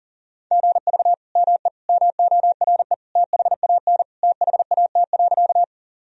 Morse code, heard as a keyed tone, about 37 wpm, 700 Hz. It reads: GV ME MORE THAN THAT.
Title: Sentences from Top 200 Words and Abbreviations 40wpm